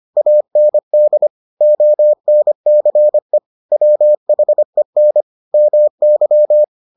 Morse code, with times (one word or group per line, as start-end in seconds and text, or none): 0.2-1.3 AND
1.6-3.4 ONCE
3.7-5.2 WHEN
5.5-6.6 MY